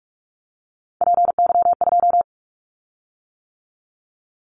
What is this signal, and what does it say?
Morse code, heard as a keyed tone, about 44 wpm, 700 Hz. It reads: PY2
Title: Callsign Prefixes for Top 50 DXCC - 45wpm